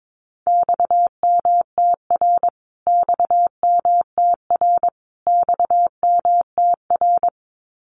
Morse code, heard as a keyed tone, about 22 wpm, 700 Hz. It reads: XMTR XMTR XMTR
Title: Abbreviations Repeated 3x - 22wpm